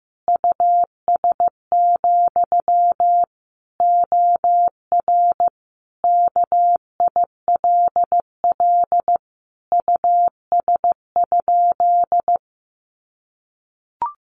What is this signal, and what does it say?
Morse code, heard as a keyed tone, about 15 wpm, 700 Hz. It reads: US, OR KILL US?
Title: Sentences from Top 1400 Words 15wpm